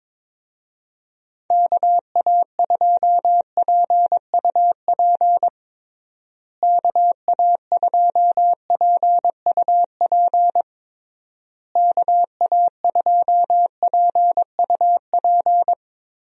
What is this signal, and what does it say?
Morse code, heard as a keyed tone, about 22 wpm, 700 Hz. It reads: KA2PUP KA2PUP KA2PUP
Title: USA Call Signs - 6 Characters - Repeated 3x - 22wpm